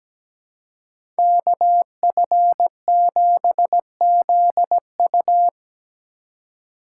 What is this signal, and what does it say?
Morse code, heard as a keyed tone, about 17 wpm, 700 Hz. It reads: KF7ZU
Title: USA Call Signs - 5 Characters - 3x Character Spacing - 17wpm